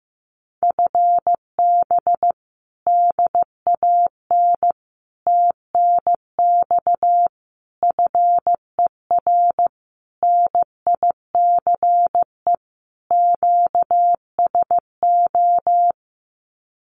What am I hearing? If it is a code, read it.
Morse code, heard as a keyed tone, about 15 wpm, 700 Hz. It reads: FB DAN TNX FER NICE QSO
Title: Common QSO Phrases Encore 15wpm